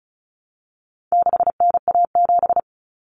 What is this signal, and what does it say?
Morse code, heard as a keyed tone, about 34 wpm, 700 Hz. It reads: 6NA7
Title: Common Cut Numbers - 4-Digit 35wpm